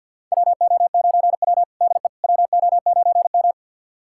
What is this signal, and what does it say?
Morse code, heard as a keyed tone, about 49 wpm, 700 Hz. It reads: WO9W DE WO9M
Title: USA Callsign DE Callsign - 4 Characters - 50wpm